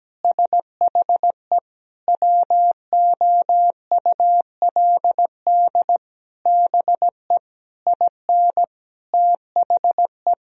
Morse code, heard as a keyed tone, about 17 wpm, 700 Hz. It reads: SHE WOULD BE IN THE
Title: Sentences from Top 1000 Words - Review Part 3 - 17wpm